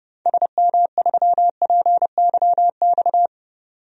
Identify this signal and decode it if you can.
Morse code, heard as a keyed tone, about 29 wpm, 700 Hz. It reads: SM3PYX
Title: DXCC-100 Call Signs - 6 Characters - 30wpm